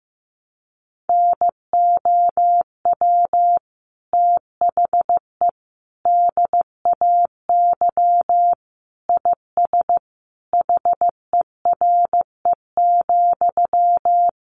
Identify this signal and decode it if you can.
Morse code, heard as a keyed tone, about 15 wpm, 700 Hz. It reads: NOW THE DAY IS HERE,
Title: Sentences from Top 1000 Words 15wpm